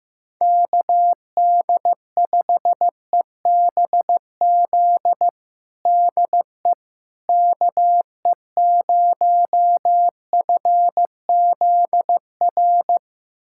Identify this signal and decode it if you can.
Morse code, heard as a keyed tone, about 16 wpm, 700 Hz. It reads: KD5EBZ DE KE0FZR
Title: USA Callsign DE Callsign - 6 Characters - 15wpm